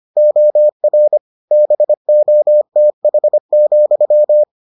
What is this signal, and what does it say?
Morse code, heard as a keyed tone, about 25 wpm, 600 Hz. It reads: OR BOTH,